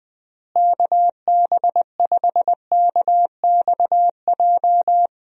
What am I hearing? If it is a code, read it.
Morse code, heard as a keyed tone, about 20 wpm, 700 Hz. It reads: KB5KXJ
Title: USA Call Signs - 6 Characters - Repeated 3x - 20wpm